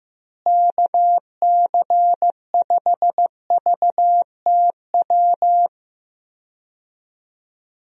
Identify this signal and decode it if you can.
Morse code, heard as a keyed tone, about 15 wpm, 700 Hz. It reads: KC5VTW
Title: USA Call Signs - 6 Characters - Repeated 2x - 15wpm